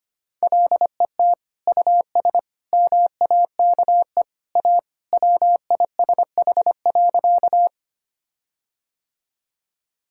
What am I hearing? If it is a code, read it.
Morse code, heard as a keyed tone, about 25 wpm, 700 Hz. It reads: LET US MAKE A WISH.
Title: Sentences from Top 1600 Words 25wpm